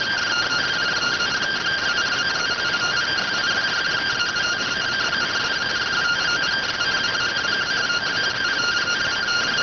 Two-tone teleprinter signal, about 170 Hz shift.